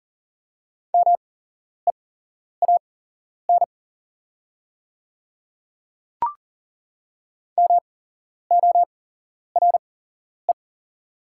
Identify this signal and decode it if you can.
Morse code, heard as a keyed tone, about 40 wpm, spaced as for 11 wpm, 700 Hz. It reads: MEAN MORE
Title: 8x Character Spacing - Top 300 Words - 40wpm